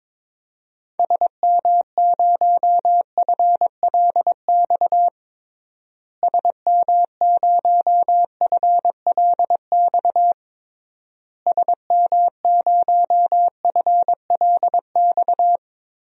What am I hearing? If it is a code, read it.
Morse code, heard as a keyed tone, about 22 wpm, 700 Hz. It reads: SM0FLX SM0FLX SM0FLX
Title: DXCC-100 Call Signs - 6 Characters - Repeated 3x - 22wpm